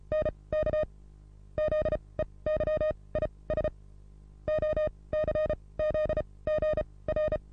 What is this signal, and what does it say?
Morse code, heard as a keyed tone, about 35 wpm, spaced as for 18 wpm, 620 Hz. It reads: NKZEY ISOCZ GR
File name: repos / Morsekurs / DJ4UF Morsen27